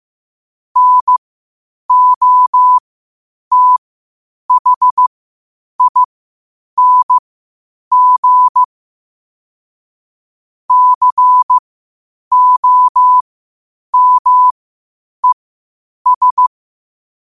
Morse code, heard as a keyed tone, about 15 wpm, spaced as for 8 wpm, 1000 Hz. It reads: NOTHING COMES
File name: qotdfiles / 8wpm